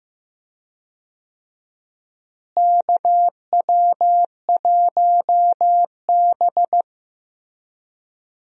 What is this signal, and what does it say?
Morse code, heard as a keyed tone, about 15 wpm, 700 Hz. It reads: KW1B
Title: USA Call Signs - 4 Characters - Repeated 3x - 15wpm